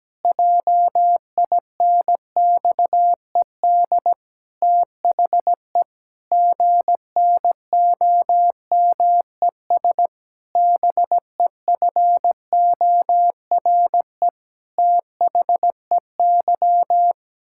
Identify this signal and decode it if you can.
Morse code, heard as a keyed tone, about 17 wpm, 700 Hz. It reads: JINXED THE GNOMES BEFORE THEY
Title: Pangram Challenge v2 - 17wpm